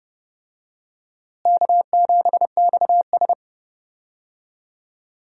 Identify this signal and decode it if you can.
Morse code, heard as a keyed tone, about 30 wpm, 700 Hz. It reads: K7XS
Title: USA Call Signs - 4 Characters - Repeated 2x - 30wpm